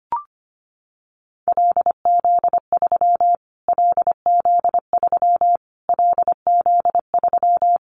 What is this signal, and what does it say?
Morse code, heard as a keyed tone, about 25 wpm, 700 Hz. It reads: LZ3 LZ3 LZ3
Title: Callsign Prefixes for Top 25 DXCC - Repeated 3x - 25wpm